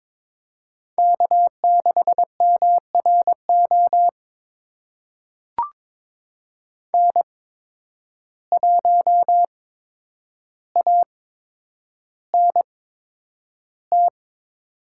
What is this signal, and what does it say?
Morse code, heard as a keyed tone, about 22 wpm, spaced as for 9 wpm, 700 Hz. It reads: K6MRO N1ANT
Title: USA Call Signs - 5 Characters - 8x Character Spacing - 22wpm